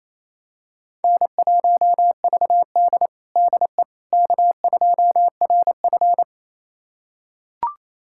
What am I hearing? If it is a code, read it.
Morse code, heard as a keyed tone, about 28 wpm, 700 Hz. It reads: N1VD DE K2RF
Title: USA Callsign DE Callsign - 4 Characters - 28wpm